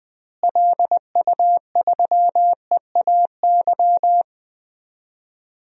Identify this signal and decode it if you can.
Morse code, heard as a keyed tone, about 20 wpm, 700 Hz. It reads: LU3EAY
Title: DXCC-100 Call Signs - 6 Characters - Repeated 3x - 20wpm